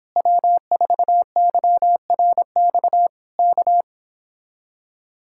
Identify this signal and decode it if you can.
Morse code, heard as a keyed tone, about 26 wpm, 700 Hz. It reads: W4YRX K
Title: Speed-Racing Common QSO Phrases - 22wpm